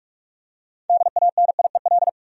Morse code, heard as a keyed tone, about 45 wpm, 700 Hz. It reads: DANIEL
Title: USA Common Names - 45wpm